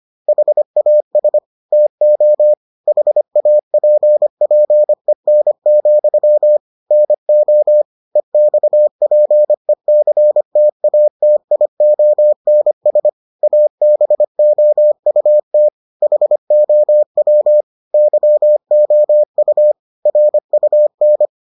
Morse code, heard as a keyed tone, about 25 wpm, 600 Hz. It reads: HAS TO HAPPEN, NO EXPECTATIONS ABOUT HOW YOU RUN